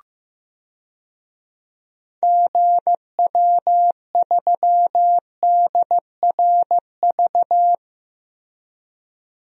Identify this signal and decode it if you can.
Morse code, heard as a keyed tone, about 15 wpm, 700 Hz. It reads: GW3DRV
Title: DXCC-100 Call Signs - 6 Characters - Repeated 3x - 15wpm